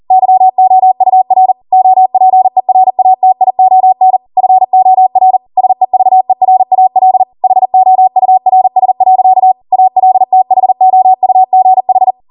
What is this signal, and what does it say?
Morse code, heard as a keyed tone, about 40 wpm, 750 Hz. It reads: YOUR OPERATION FOR SEVERAL HOURS. ALTHOUGH
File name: Morse / 40WPM